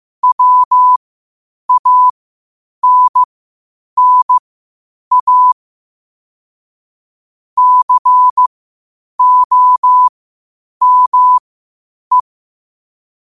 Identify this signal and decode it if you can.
Morse code, heard as a keyed tone, about 15 wpm, spaced as for 8 wpm, 1000 Hz. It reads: WANNA COME
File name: qotdfiles / 8wpm